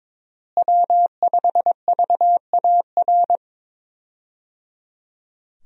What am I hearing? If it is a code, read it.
Morse code, heard as a keyed tone, about 23 wpm, 700 Hz. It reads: W5VAR
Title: USA Call Signs - 5 Characters - Repeated 3x - 22wpm